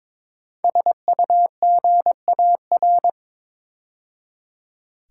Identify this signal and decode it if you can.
Morse code, heard as a keyed tone, about 22 wpm, 700 Hz. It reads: SUGAR
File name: 5 Letter Words - First 3 Letters Given - 22wpm